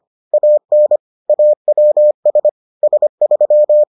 Morse code, heard as a keyed tone, about 25 wpm, 600 Hz. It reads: AN AWS S3